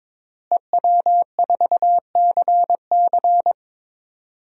Morse code, heard as a keyed tone, about 22 wpm, 700 Hz. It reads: EW4CC
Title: DXCC-100 Call Signs - 5 Characters - Repeated 3x - 22wpm